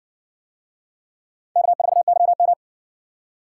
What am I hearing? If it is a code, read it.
Morse code, heard as a keyed tone, about 58 wpm, 700 Hz. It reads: K4YM